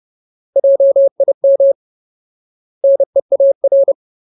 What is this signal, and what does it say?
Morse code, heard as a keyed tone, about 30 wpm, 550 Hz. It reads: JIM NEAR
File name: Ultimate Word List / 30WPM Ultimate